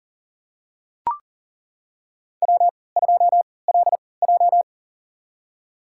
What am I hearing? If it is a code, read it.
Morse code, heard as a keyed tone, about 38 wpm, spaced as for 22 wpm, 700 Hz. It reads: W2LJ
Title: USA Call Signs - 4 Characters - 3x Character Spacing - 40wpm